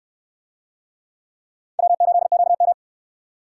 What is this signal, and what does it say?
Morse code, heard as a keyed tone, about 65 wpm, 700 Hz. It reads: W8QM